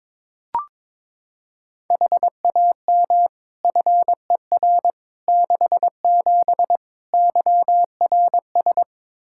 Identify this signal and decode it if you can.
Morse code, heard as a keyed tone, about 22 wpm, 700 Hz. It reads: HAM FER 67 YRS
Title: Practice QSO - Exchange 3 - More Info - 22wpm